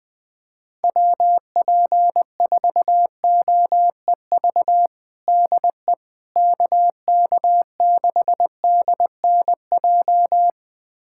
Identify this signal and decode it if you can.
Morse code, heard as a keyed tone, about 21 wpm, 700 Hz. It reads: WP4OEV DE KK6DNJ
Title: USA Callsign DE Callsign - 6 Characters - 20wpm